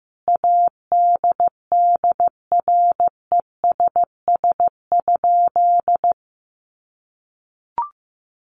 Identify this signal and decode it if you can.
Morse code, heard as a keyed tone, about 14 wpm, 700 Hz. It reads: ADDRESS?
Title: Sentences from Top 1000 Words 15wpm